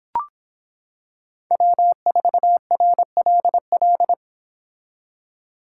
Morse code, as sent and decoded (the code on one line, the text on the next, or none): .-- ....- .-. .-.. .-..
W4RLL